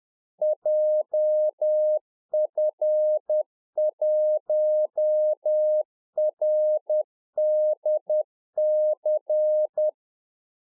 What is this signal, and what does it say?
Morse code, heard as a keyed tone, about 10 wpm, 610 Hz. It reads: JF1RDC
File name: 2011-04-mp3 / NEW MEMBERS 10WPM